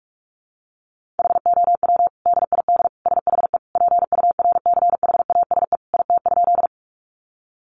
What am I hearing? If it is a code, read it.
Morse code, heard as a keyed tone, about 44 wpm, 700 Hz. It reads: HOW DID SHE PURCHASE IT?
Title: Sentences from Top 800 Words 45wpm